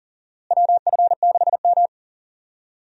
Morse code, heard as a keyed tone, about 38 wpm, 700 Hz. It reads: WF6M
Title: USA Call Signs - 4 Characters - Repeated 3x - 40wpm